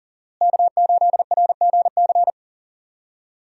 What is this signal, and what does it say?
Morse code, heard as a keyed tone, about 40 wpm, 700 Hz. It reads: K8RGC
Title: USA Call Signs - 5 Characters - Repeated 2x - 40wpm